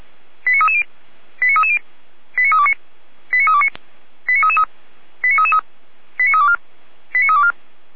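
Five-tone selective call (ZVEI), heard as quick runs of five tones, tones of about 70 ms. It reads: 89330 89330 89329 89329 89333 89333 89324 89324
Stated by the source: ZVEI five-tone selective call received on an SDR, from the Feuerwehr Orlamünde audio archive.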